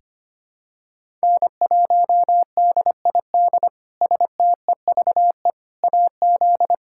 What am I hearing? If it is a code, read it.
Morse code, heard as a keyed tone, about 25 wpm, 700 Hz. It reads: N1DID STEVE AZ
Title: K1USN SST QSO Practice - 25wpm